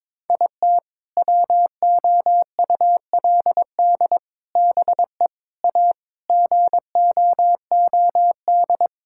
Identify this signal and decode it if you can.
Morse code, heard as a keyed tone, about 22 wpm, 700 Hz. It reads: IT WOULD BE A GOOD
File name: practice / 22wpm / Sentences with 3-Word Phrases - 22wpm